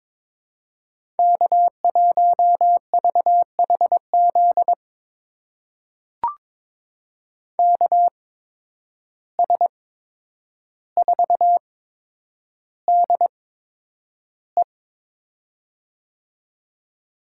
Morse code, heard as a keyed tone, about 22 wpm, spaced as for 9 wpm, 700 Hz. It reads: K1VHZ KS4DE
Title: USA Call Signs - 5 Characters - 8x Character Spacing - 22wpm